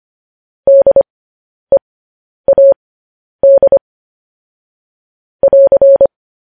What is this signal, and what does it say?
Morse code, heard as a keyed tone, about 25 wpm, spaced as for 10 wpm, 550 Hz. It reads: DEAD <AR>
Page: news headlines in morse code at 10 wpm